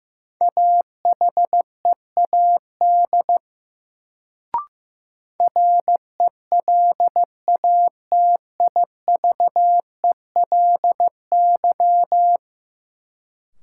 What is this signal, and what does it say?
Morse code, heard as a keyed tone, about 16 wpm, 700 Hz. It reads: AHEAD RELATIVELY
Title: Sentences from Top 1100 Words - New Word Drill Alt - 15wpm